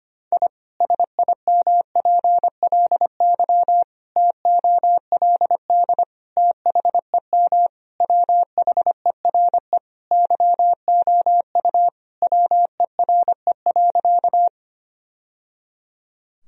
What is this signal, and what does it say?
Morse code, heard as a keyed tone, about 25 wpm, 700 Hz. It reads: I SIMPLY TOLD THEM WHERE YOU WERE.
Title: Sentences from Top 1000 Words - Review Part 4 - 25wpm